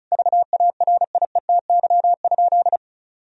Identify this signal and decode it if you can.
Morse code, heard as a keyed tone, about 36 wpm, 700 Hz. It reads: VARIETY?
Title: Sentences from Top 1000 Words - Review Part 4 - 35wpm